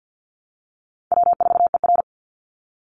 Morse code, heard as a keyed tone, about 48 wpm, 700 Hz. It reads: W4ER
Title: Call Signs Repeated 3x - 50wpm